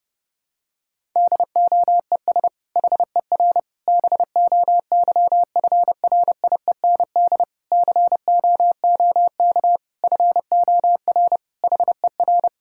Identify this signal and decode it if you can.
Morse code, heard as a keyed tone, about 32 wpm, 700 Hz. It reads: DOES HER BOYFRIEND COOK FOR HER
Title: Sentences from Top 1000 Words - Review Part 5 - 30wpm